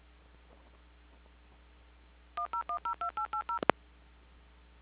DTMF, heard as telephone touch tones, tones of about 110 ms. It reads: 10102880